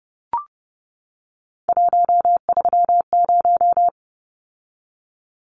Morse code, heard as a keyed tone, about 30 wpm, 700 Hz. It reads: 130
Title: Numbers - 3-Digit 30wpm